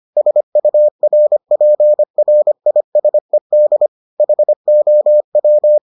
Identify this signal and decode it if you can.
Morse code, heard as a keyed tone, about 25 wpm, 600 Hz. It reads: SURPRISED HOW